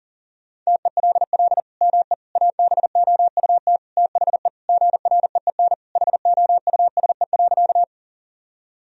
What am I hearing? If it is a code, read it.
Morse code, heard as a keyed tone, about 40 wpm, 700 Hz. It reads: TELL ME ABOUT THE GREEN HOUSE.
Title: Sentences from Top 1000 Words - Review Part 2 - 40wpm